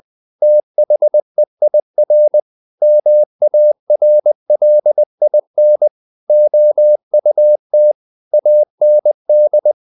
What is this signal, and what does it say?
Morse code, heard as a keyed tone, about 20 wpm, 600 Hz. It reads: THEIR MARLIN OUT AND
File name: the old man and the sea / the old man and the sea part 1 20wpm